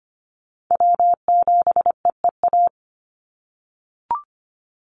Morse code, heard as a keyed tone, about 25 wpm, 700 Hz. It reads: W7EEA
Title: USA Call Signs 25wpm